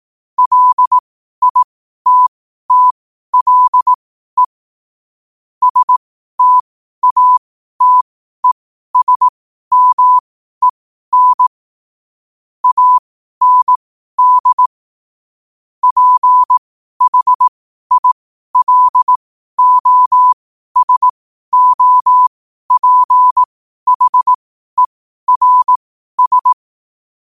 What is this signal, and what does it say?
Morse code, heard as a keyed tone, about 18 wpm, spaced as for 12 wpm, 1000 Hz. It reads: LITTLE STATESMEN AND PHILOSOPHERS
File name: qotdfiles / 12wpm